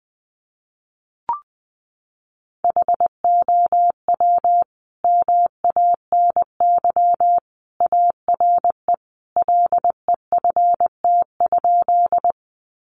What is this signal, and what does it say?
Morse code, heard as a keyed tone, about 20 wpm, 700 Hz. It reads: HOW MANY ARE LEFT?
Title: Sentences from Top 300 Words 20wpm